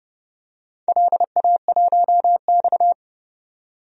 Morse code, heard as a keyed tone, about 30 wpm, 700 Hz. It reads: LA1X
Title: DXCC-100 Call Signs - 4 Characters - 30wpm